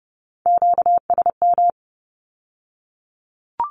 Morse code, heard as a keyed tone, about 31 wpm, 700 Hz. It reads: QSM
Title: Q-Codes with Spoken Meaning - 30wpm